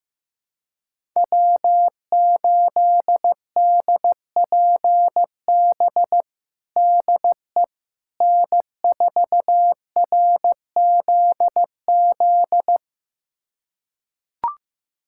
Morse code, heard as a keyed tone, about 15 wpm, 700 Hz. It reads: W8DPB DE N4RZZ
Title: USA Callsign DE Callsign - 5 Characters - 15wpm